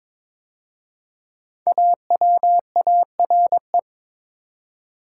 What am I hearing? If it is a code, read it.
Morse code, heard as a keyed tone, about 22 wpm, 700 Hz. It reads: AWARE
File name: 5 Letter Words - First 3 Letters Given - 22wpm